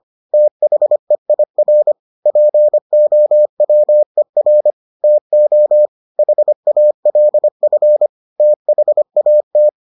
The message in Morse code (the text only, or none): THEIR POWER TO HALF THAT